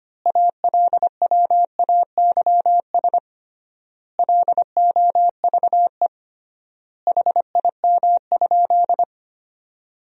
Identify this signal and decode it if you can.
Morse code, heard as a keyed tone, about 25 wpm, 700 Hz. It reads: ALWAYS LOVE HIM?
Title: 3x Word-Spacing - Sentences from Top 400 Words - 25wpm